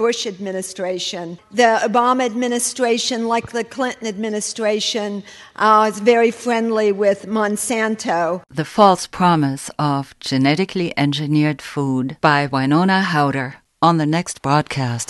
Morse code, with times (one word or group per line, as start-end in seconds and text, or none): none